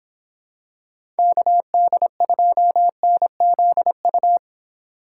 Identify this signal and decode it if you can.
Morse code, heard as a keyed tone, about 25 wpm, 700 Hz. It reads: KD2NZU